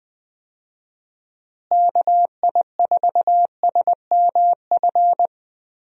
Morse code, heard as a keyed tone, about 20 wpm, 700 Hz. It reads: KI4SMF
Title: USA Call Signs - 6 Characters - Repeated 3x - 20wpm